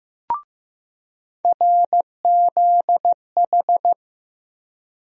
Morse code, heard as a keyed tone, about 15 wpm, 700 Hz. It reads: RZH
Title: USA Call Sign Postfixes - 3 Letters 15wpm